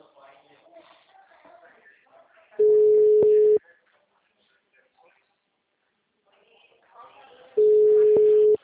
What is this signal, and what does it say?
Ringback tone (Europe, 425 Hz), with speech.